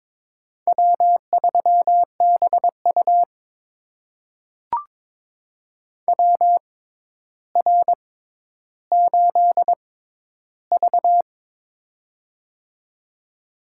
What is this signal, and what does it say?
Morse code, heard as a keyed tone, about 21 wpm, spaced as for 11 wpm, 700 Hz. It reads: W3BU WR8V
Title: USA Call Signs - 4 Characters - 6x Character Spacing - 22wpm